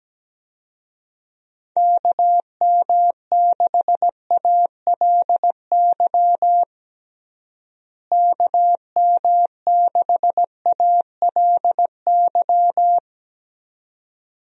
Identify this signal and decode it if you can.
Morse code, heard as a keyed tone, about 17 wpm, 700 Hz. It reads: KM6ALY KM6ALY
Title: USA Call Signs - 6 Characters - Repeated 2x - 17wpm